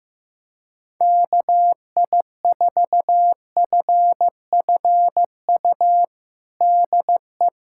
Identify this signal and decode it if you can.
Morse code, heard as a keyed tone, about 15 wpm, 700 Hz. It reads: KI4FFU DE
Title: USA Callsign DE Callsign 15wpm